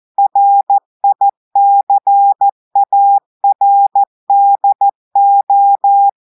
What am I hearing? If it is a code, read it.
Morse code, heard as a keyed tone, about 14 wpm, 800 Hz. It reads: RICARDO
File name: audios / portugais / portugais texte en clair 1 vitesse 14 mots minutes